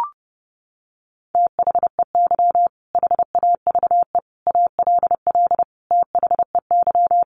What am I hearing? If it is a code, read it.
Morse code, heard as a keyed tone, about 30 wpm, 700 Hz. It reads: THEY HAVE ALL THEY
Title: Sentences from Top 1100 Words 30wpm